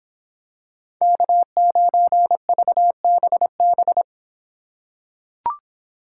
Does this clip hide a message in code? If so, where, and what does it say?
Yes: Morse code from 1.0 to 4.0 s, heard as a keyed tone, about 25 wpm, 700 Hz. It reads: K9VBB